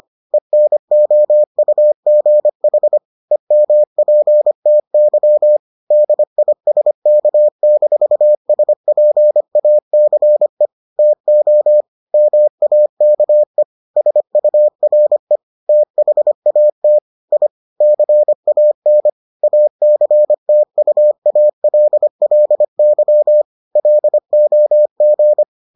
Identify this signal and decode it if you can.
Morse code, heard as a keyed tone, about 25 wpm, 600 Hz. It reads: ENOUGH EMPTY DISK-SPACE TO MAKE SURE THAT I CAN ACTUALLY LOG